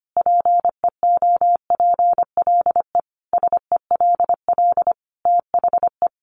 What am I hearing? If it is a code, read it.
Morse code, heard as a keyed tone, about 25 wpm, 700 Hz. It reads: PEOPLE SELL THE
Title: Sentences from Top 1100 Words 25wpm